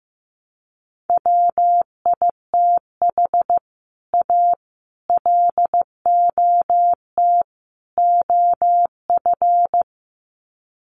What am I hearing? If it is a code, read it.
Morse code, heard as a keyed tone, about 15 wpm, 700 Hz. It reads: WITH A LOT OF
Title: Sets of 4 Words 15wpm